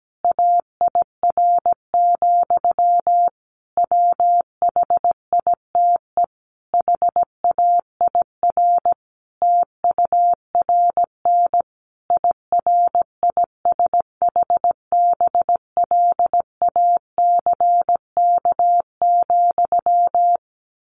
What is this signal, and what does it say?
Morse code, heard as a keyed tone, about 17 wpm, 700 Hz. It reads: AIR, WHITE HAIR TURN IRISHBLACK,
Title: A Sound Of Thunder 17wpm